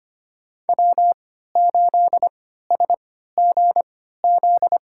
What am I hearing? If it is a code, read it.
Morse code, heard as a keyed tone, about 25 wpm, spaced as for 14 wpm, 700 Hz. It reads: W8SGZ
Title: USA Call Signs - 5 Characters - 3x Character Spacing - 25wpm